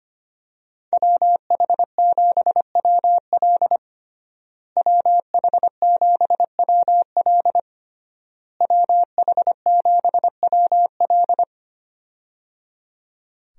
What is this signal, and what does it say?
Morse code, heard as a keyed tone, about 25 wpm, 700 Hz. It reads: WH7WL WH7WL WH7WL
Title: USA Call Signs - 5 Characters - Repeated 3x - 25wpm